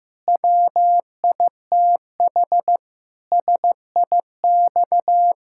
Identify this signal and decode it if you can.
Morse code, heard as a keyed tone, about 15 wpm, 700 Hz. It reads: WITH SIX
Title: Pangram Challenge v2 - 15wpm